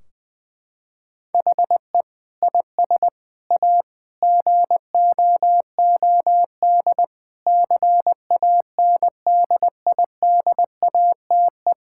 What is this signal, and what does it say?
Morse code, heard as a keyed tone, about 21 wpm, 700 Hz. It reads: HE IS A GOOD CANDIDATE
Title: Sentences from Top 1600 Words 20wpm